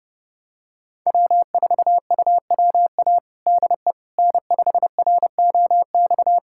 Morse code, heard as a keyed tone, about 31 wpm, 700 Hz. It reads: W4UWA DE N5ROX
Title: USA Callsign DE Callsign - 5 Characters - 30wpm